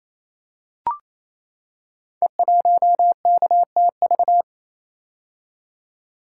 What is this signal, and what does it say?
Morse code, heard as a keyed tone, about 28 wpm, 700 Hz. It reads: E1KTV
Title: Code Groups - 5 Letter:Numbers - 28wpm